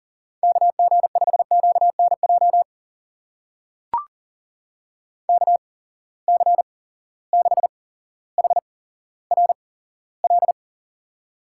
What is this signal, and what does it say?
Morse code, heard as a keyed tone, about 40 wpm, spaced as for 17 wpm, 700 Hz. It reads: KG5QNJ KC6HRL
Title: USA Call Signs - 6 Characters - 8x Character Spacing - 40wpm